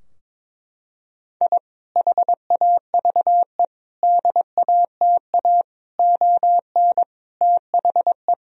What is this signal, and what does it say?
Morse code, heard as a keyed tone, about 22 wpm, 700 Hz. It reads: I HAVE DATA ON THE
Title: Sentences from Top 1000 Words - Review Part 3 - 22wpm